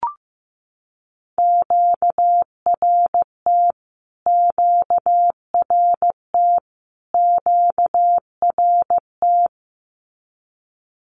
Morse code, heard as a keyed tone, about 15 wpm, 700 Hz. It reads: QRT QRT QRT
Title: Q-Codes Repeated 3x 15wpm